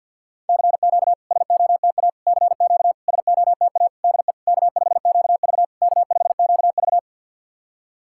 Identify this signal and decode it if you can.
Morse code, heard as a keyed tone, about 52 wpm, 690 Hz. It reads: CQ SOTA CQ SOTA DE K5QV K5QV
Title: SOTA QSO Practice 50wpm